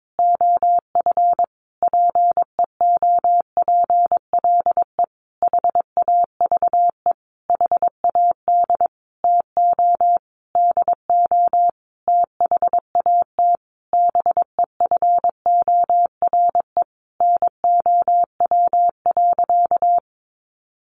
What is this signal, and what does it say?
Morse code, heard as a keyed tone, about 22 wpm, 700 Hz. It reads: OF PEOPLE HAVE HAD TO DO THAT BEFORE NOW.
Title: Sentences from Top 1300 Words 22wpm